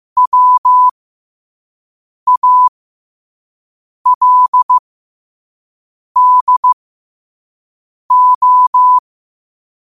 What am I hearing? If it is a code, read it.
Morse code, heard as a keyed tone, about 15 wpm, spaced as for 5 wpm, 1000 Hz. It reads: WALDO
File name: qotdfiles / 5wpm